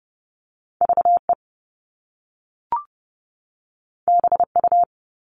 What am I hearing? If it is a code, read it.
Morse code, heard as a keyed tone, about 30 wpm, 700 Hz. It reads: VE BU
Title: All Cut Numbers - 2-Digit 30wpm